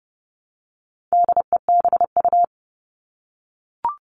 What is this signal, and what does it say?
Morse code, heard as a keyed tone, about 30 wpm, 700 Hz. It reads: DEBU
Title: All Cut Numbers - 4-Digit 30wpm